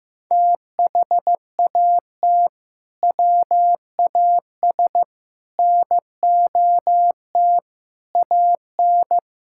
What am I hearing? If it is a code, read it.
Morse code, heard as a keyed tone, about 15 wpm, 700 Hz. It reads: THAT WAS NOT AN
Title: Sentences from Top 1000 Words - Review Part 5 - 15wpm